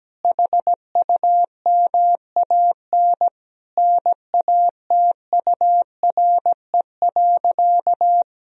Morse code, heard as a keyed tone, about 17 wpm, 700 Hz. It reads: HUMAN NATURE.
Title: Sentences from Top 1000 Words - Review Part 3 - 17wpm